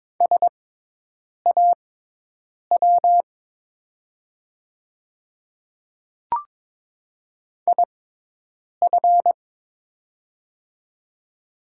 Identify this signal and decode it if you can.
Morse code, heard as a keyed tone, about 22 wpm, spaced as for 8 wpm, 700 Hz. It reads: SAW IF
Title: 6x Character Spacing - Top 300 Words - 22wpm